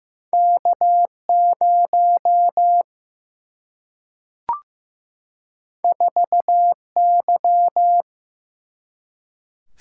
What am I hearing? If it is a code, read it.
Morse code, heard as a keyed tone, about 15 wpm, 700 Hz. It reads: K 0 4 Y
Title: Single-Letter Single-Number 15wpm